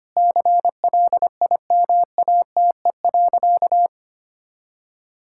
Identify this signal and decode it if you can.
Morse code, heard as a keyed tone, about 25 wpm, 700 Hz. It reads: CLIMATE.
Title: Sentences from Top 1600 Words 25wpm